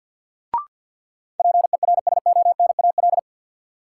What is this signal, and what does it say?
Morse code, heard as a keyed tone, about 48 wpm, 700 Hz. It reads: PERSONAL
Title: Sentences from Top 700 Words - New Word Drill - 50wpm